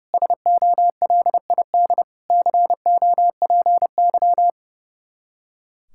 Morse code, heard as a keyed tone, about 32 wpm, 700 Hz. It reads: SOLID COPY
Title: Practice QSO - Exchange 3 - More Info - 30wpm